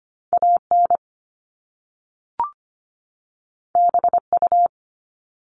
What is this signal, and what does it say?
Morse code, heard as a keyed tone, about 25 wpm, 700 Hz. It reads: AN BU
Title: Top Two Letter Combinations in Words 25wpm